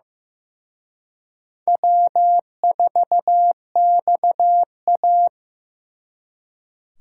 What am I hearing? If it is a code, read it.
Morse code, heard as a keyed tone, about 18 wpm, 700 Hz. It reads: W4XA
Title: Speed Racing - USA 1x2 Call Signs 15wpm